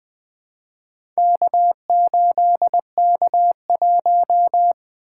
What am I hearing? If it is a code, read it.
Morse code, heard as a keyed tone, about 20 wpm, 700 Hz. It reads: K8K1
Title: Four Letter-Number Combinations - 20wpm